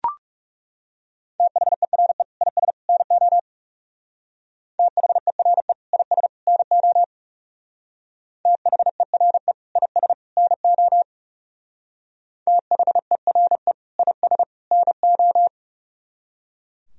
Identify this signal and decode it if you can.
Morse code, heard as a keyed tone, about 37 wpm, 700 Hz. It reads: THERE IS NO THERE IS NO THERE IS NO THERE IS NO
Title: Speed-Racing - Sets of 3 Words 30wpm